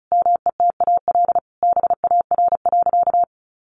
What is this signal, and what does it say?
Morse code, heard as a keyed tone, about 34 wpm, 700 Hz. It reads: METAL BAR.
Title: Additional Easy Phrases 35wpm